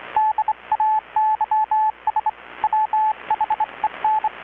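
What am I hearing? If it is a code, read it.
Morse code, heard as a keyed tone, about 25 wpm, 860 Hz, with radio noise. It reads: DAYS WHEN
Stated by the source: Morse from KPH's Night of Nights XX maritime radiotelegraphy event, received in Chiapas, Mexico about 3,800 km away on a Kenwood R-600 with a grounded 10 m long-wire antenna.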